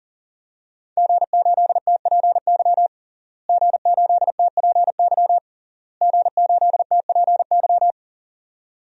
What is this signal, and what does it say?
Morse code, heard as a keyed tone, about 40 wpm, 700 Hz. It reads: G8TPY G8TPY G8TPY
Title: DXCC-100 Call Signs - 5 Characters - Repeated 3x - 40wpm